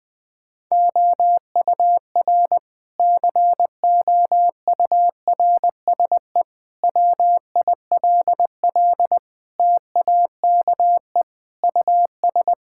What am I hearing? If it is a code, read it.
Morse code, heard as a keyed tone, about 20 wpm, 700 Hz. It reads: OUR COURSE WILL TAKE US
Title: Sentences from Top 1000 Words - Review Part 6 - 20wpm